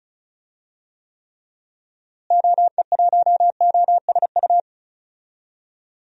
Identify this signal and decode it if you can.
Morse code, heard as a keyed tone, about 36 wpm, 700 Hz. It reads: OE1OSU
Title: DXCC-100 Call Signs - 6 Characters - Repeated 2x - 35wpm